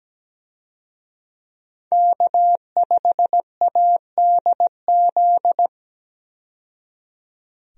Morse code, heard as a keyed tone, about 17 wpm, 700 Hz. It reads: K5ADZ
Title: USA Call Signs - 5 Characters - 17wpm